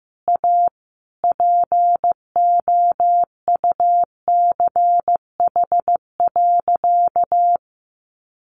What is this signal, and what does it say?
Morse code, heard as a keyed tone, about 14 wpm, 700 Hz. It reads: A POUCH.
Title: Easy Phrases 15wpm